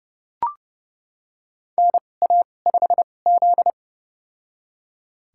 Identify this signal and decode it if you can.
Morse code, heard as a keyed tone, about 30 wpm, spaced as for 22 wpm, 700 Hz. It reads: NA5Z
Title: USA Call Signs - 4 Characters - 2x Character Spacing - 30wpm